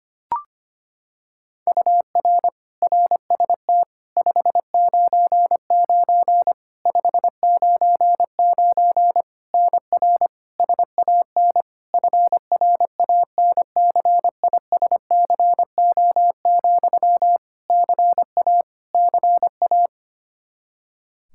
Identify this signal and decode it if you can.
Morse code, heard as a keyed tone, about 25 wpm, 700 Hz. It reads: UR RST 599 599 NR SAN FRANCISCO, CA CA
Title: Practice QSO - Exchange 1 - Beginning - 25wpm